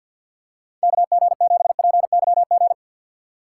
Morse code, heard as a keyed tone, about 52 wpm, 700 Hz. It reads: KG7PYG
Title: USA Call Signs - 4x Character Spacing - 50wpm